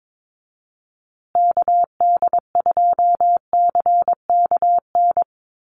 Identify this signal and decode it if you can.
Morse code, heard as a keyed tone, about 22 wpm, 700 Hz. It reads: KD2CKN
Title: USA Call Signs Repeated 2x 22wpm